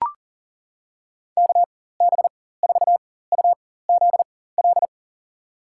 Morse code, heard as a keyed tone, about 40 wpm, spaced as for 19 wpm, 700 Hz. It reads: KB4UZL
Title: USA Call Signs - 6 Characters - 4x Character Spacing - 40wpm